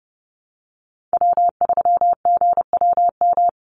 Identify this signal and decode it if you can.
Morse code, heard as a keyed tone, about 30 wpm, 700 Hz. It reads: W3GWM
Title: Call Signs Repeated 2x - 30wpm